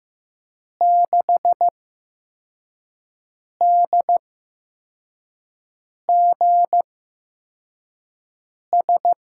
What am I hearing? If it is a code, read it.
Morse code, heard as a keyed tone, about 15 wpm, spaced as for 4 wpm, 700 Hz. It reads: 6DGS
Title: USA Call Signs - 5 Characters - 8x Character Spacing - 15wpm